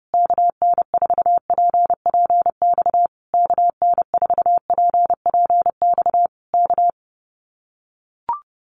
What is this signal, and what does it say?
Morse code, heard as a keyed tone, about 30 wpm, 700 Hz. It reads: KN4PPX KN4PPX K
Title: Common QSO Phrases Encore 30wpm